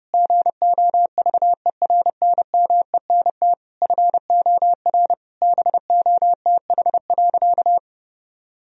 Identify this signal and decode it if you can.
Morse code, heard as a keyed tone, about 30 wpm, 700 Hz. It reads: GOVERNMENT FOR BOTH.
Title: Sentences from Top 1000 Words - Review Part 3 - 30wpm